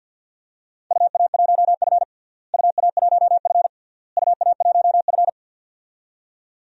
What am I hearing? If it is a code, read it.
Morse code, heard as a keyed tone, about 52 wpm, 700 Hz. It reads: UA1F UA1F UA1F
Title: DXCC-100 Call Signs - 4 Characters - Repeated 3x - 50wpm